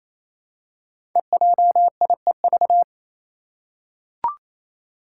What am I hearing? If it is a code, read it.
Morse code, heard as a keyed tone, about 27 wpm, 700 Hz. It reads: EJIEV
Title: Five Challenging Letter-Number Combinations 28wpm